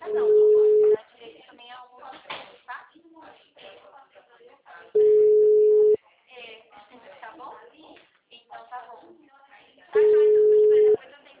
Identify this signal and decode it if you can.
Ringback tone (Europe, 425 Hz), with speech.